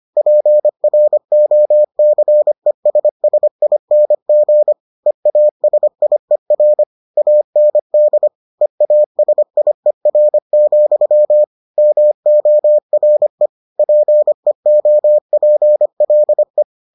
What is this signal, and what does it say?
Morse code, heard as a keyed tone, about 25 wpm, 600 Hz. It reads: PROCESSING EASIER AND EASIER, MORE PEOPLE